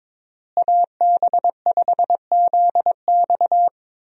Morse code, heard as a keyed tone, about 22 wpm, 700 Hz. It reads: AB5ZX
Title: USA Call Signs - 5 Characters - Repeated 3x - 22wpm